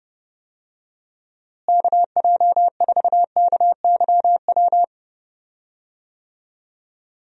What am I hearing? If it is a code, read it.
Morse code, heard as a keyed tone, about 30 wpm, 700 Hz. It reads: KJ4KYW
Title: USA Call Signs - 6 Characters - Repeated 3x - 30wpm